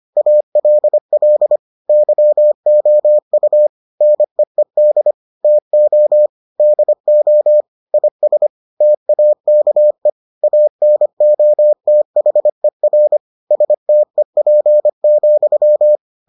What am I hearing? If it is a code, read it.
Morse code, heard as a keyed tone, about 25 wpm, 600 Hz. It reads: ALL YOU NEED TO DO IS TAKE ANOTHER STEP,